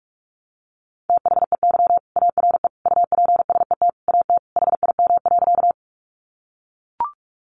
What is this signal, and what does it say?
Morse code, heard as a keyed tone, about 44 wpm, 700 Hz. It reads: THEY ARE UPSET AT HIM.
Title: Sentences from Top 800 Words 45wpm